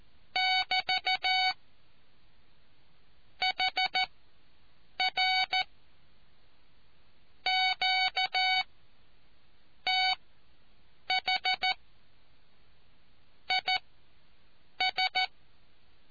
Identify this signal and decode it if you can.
Morse code, heard as a keyed tone, about 14 wpm, spaced as for 6 wpm, 750 Hz. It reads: <BT> HR QTH IS